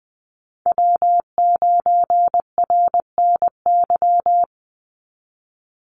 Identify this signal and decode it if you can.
Morse code, heard as a keyed tone, about 19 wpm, 700 Hz. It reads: W9RNY
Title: Call Signs - 20wpm